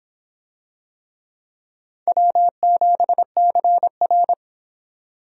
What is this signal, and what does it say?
Morse code, heard as a keyed tone, about 25 wpm, 700 Hz. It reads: W7CR